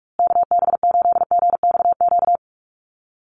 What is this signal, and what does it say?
Morse code, heard as a keyed tone, about 44 wpm, 700 Hz. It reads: KB8GXQ
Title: USA Call Signs Repeated 2x 45wpm